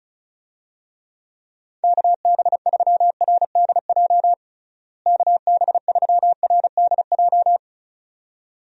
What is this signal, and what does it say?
Morse code, heard as a keyed tone, about 34 wpm, 700 Hz. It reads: KB3RDJ KB3RDJ
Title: USA Call Signs - 6 Characters - Repeated 2x - 35wpm